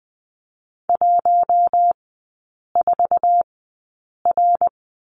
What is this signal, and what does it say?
Morse code, heard as a keyed tone, about 20 wpm, spaced as for 8 wpm, 700 Hz. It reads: 14R
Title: 053 Lesson 10e - Letter r - Three Characters 20wpm